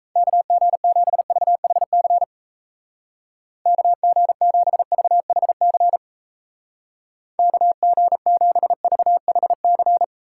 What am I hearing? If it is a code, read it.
Morse code, heard as a keyed tone, about 37 wpm, 700 Hz. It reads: KG7VHC KG7VHC KG7VHC